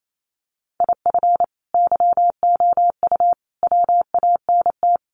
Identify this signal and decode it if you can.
Morse code, heard as a keyed tone, about 28 wpm, 700 Hz. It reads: IF YOU WANT
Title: Sets of 3 Words 28wpm